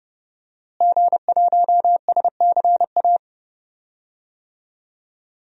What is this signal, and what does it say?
Morse code, heard as a keyed tone, about 30 wpm, 700 Hz. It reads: G1SCA
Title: DXCC-100 Call Signs - 5 Characters - Repeated 3x - 30wpm